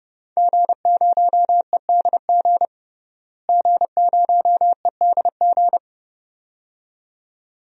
Morse code, heard as a keyed tone, about 30 wpm, 700 Hz. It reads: G0EDG G0EDG
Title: DXCC-100 Call Signs - 5 Characters - Repeated 2x - 30wpm